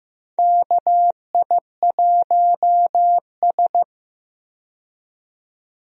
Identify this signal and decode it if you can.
Morse code, heard as a keyed tone, about 15 wpm, 700 Hz. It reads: KI1S
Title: USA Call Signs - 4 Characters - 15wpm